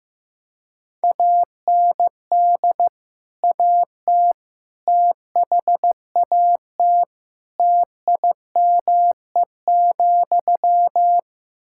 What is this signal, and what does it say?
Morse code, heard as a keyed tone, about 16 wpm, 700 Hz. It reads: AND AT THAT TIME,
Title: Sentences from Top 1000 Words - Review Part 5 - 15wpm